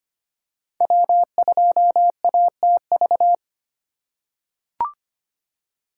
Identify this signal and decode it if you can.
Morse code, heard as a keyed tone, about 25 wpm, 700 Hz. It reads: W2ATV
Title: USA Call Signs - 5 Characters - 3x Character Spacing - 25wpm